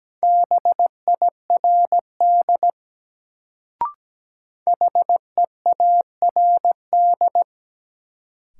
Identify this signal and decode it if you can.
Morse code, heard as a keyed tone, about 17 wpm, 700 Hz. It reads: BIRD HEARD
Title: Sentences from Top 400 Words - New Word Drill - 17wpm